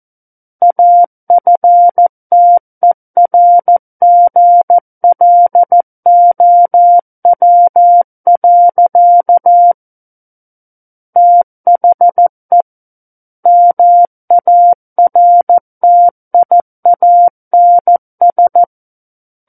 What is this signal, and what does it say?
Morse code, heard as a keyed tone, about 14 wpm, 700 Hz. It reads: AFTERGLOW. THE MARTIANS
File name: Book One / 05-Chapter 5 - 14 WPM